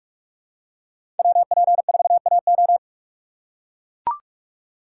Morse code, heard as a keyed tone, about 42 wpm, 700 Hz. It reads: WP4AO
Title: USA Call Signs - 5 Characters - 3x Character Spacing - 45wpm